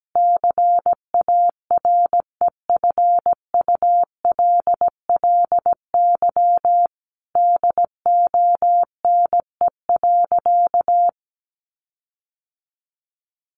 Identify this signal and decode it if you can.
Morse code, heard as a keyed tone, about 17 wpm, 700 Hz. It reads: CAREFULLY DONE.
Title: Sentences from Top 1300 Words 17wpm